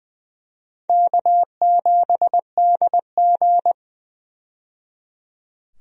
Morse code, heard as a keyed tone, about 20 wpm, 700 Hz. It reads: K7DG
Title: USA Call Signs - 4 Characters - 20wpm